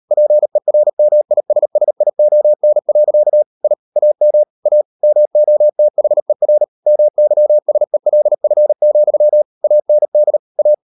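Morse code, heard as a keyed tone, about 38 wpm, 600 Hz. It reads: PERMISSION. I AM A MOTHER MYSELF, AND A
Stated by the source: Morse code practice at 38 WPM, sending Ken Follett's novel Never in English, from CW med Gnister.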